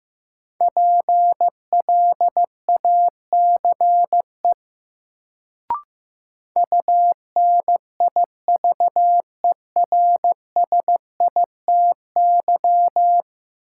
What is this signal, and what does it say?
Morse code, heard as a keyed tone, about 15 wpm, 700 Hz. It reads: PLACE UNIVERSITY
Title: Sets of 4 Words - New Word Drill - 15wpm